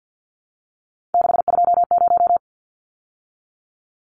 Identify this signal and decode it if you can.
Morse code, heard as a keyed tone, about 49 wpm, 700 Hz. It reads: 620
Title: Numbers - 3-Digit 50wpm